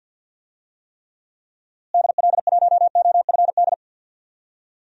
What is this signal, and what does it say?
Morse code, heard as a keyed tone, about 51 wpm, 700 Hz. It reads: DL1OFD
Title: DXCC-100 Call Signs - 6 Characters - Repeated 3x - 50wpm